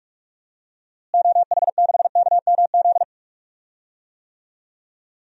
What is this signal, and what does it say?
Morse code, heard as a keyed tone, about 46 wpm, 700 Hz. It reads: OH6KMZ
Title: DXCC-100 Call Signs - 6 Characters - Repeated 3x - 45wpm